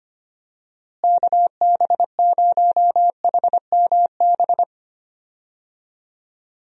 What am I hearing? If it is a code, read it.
Morse code, heard as a keyed tone, about 25 wpm, 700 Hz. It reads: KB0HMB
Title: USA Call Signs - 6 Characters - Repeated 2x - 25wpm